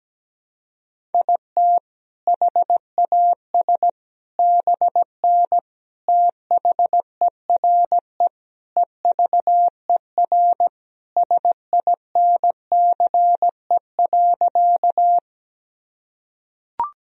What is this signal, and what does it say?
Morse code, heard as a keyed tone, about 17 wpm, 700 Hz. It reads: IT HAS BN THERE EVER SINCE.
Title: Sentences from Top 400 Words and Abbreviations 17wpm